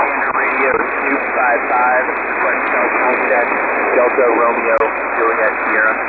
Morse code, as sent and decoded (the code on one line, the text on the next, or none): none